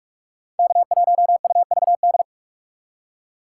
Morse code, heard as a keyed tone, about 44 wpm, 700 Hz. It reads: K1UVD